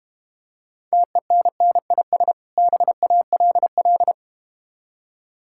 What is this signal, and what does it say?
Morse code, heard as a keyed tone, about 32 wpm, 690 Hz. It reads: TENNIS BALL